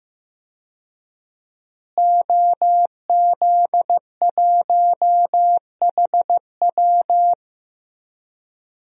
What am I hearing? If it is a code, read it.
Morse code, heard as a keyed tone, about 15 wpm, 700 Hz. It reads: OZ1HW
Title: DXCC-100 Call Signs - 5 Characters - Repeated 3x - 15wpm